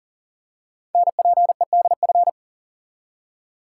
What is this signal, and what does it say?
Morse code, heard as a keyed tone, about 41 wpm, 700 Hz. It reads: NPEDF
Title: Code Groups - 5 Letter:Numbers - 40wpm